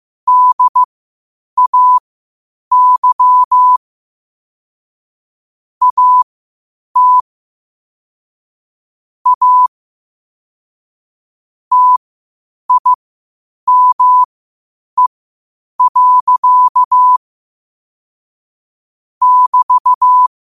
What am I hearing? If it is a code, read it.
Morse code, heard as a keyed tone, about 15 wpm, spaced as for 8 wpm, 1000 Hz. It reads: DAY AT A TIME. <BT>
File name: qotdfiles / 8wpm